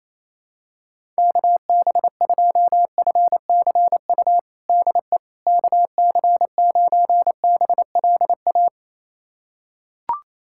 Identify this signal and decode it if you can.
Morse code, heard as a keyed tone, about 28 wpm, 700 Hz. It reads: KB2FCU DE KC9BLA
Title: USA Callsign DE Callsign 28wpm